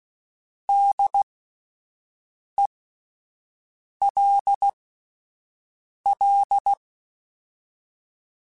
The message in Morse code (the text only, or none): DELL